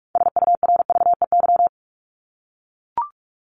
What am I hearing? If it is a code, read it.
Morse code, heard as a keyed tone, about 44 wpm, 700 Hz. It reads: SURVEY
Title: Top 1501-1750 Words 45wpm